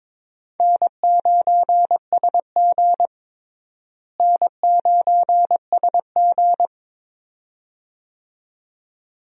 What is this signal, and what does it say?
Morse code, heard as a keyed tone, about 22 wpm, 700 Hz. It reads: N9SG N9SG
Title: USA Call Signs - 4 Characters - Repeated 2x - 22wpm